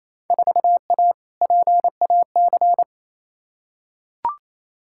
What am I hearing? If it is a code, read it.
Morse code, heard as a keyed tone, about 28 wpm, 700 Hz. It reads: 4A PAC
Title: Field Day Exchanges Encore - 28wpm